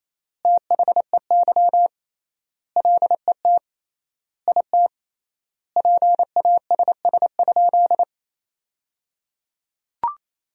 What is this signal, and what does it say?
Morse code, heard as a keyed tone, about 28 wpm, 700 Hz. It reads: THEY LET IT PASS?
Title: 3x Word-Spacing - Sentences from Top 400 Words - 28wpm